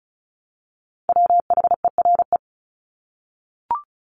Morse code, heard as a keyed tone, about 34 wpm, 700 Hz. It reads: WHERE
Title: Top 300 Words 35wpm